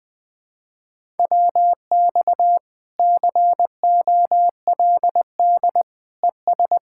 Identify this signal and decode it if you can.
Morse code, heard as a keyed tone, about 20 wpm, 700 Hz. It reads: WX COLD ES
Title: Speed-Racing Common QSO Phrases Encore - 17wpm